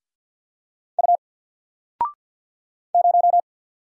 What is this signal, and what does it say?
Morse code, heard as a keyed tone, about 48 wpm, 700 Hz. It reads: U 0
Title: Single Letter-Number Alt 50wpm